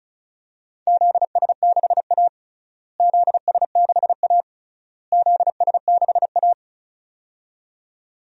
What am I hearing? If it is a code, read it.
Morse code, heard as a keyed tone, about 36 wpm, 700 Hz. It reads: ZS6A ZS6A ZS6A
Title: DXCC-100 Call Signs - 4 Characters - Repeated 3x - 35wpm